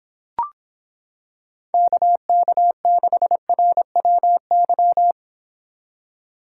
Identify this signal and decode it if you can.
Morse code, heard as a keyed tone, about 26 wpm, 700 Hz. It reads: KK6RWY